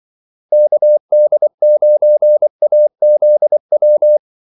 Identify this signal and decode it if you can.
Morse code, heard as a keyed tone, about 24 wpm, 600 Hz. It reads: KD9AZW
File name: Callsigns-24wpm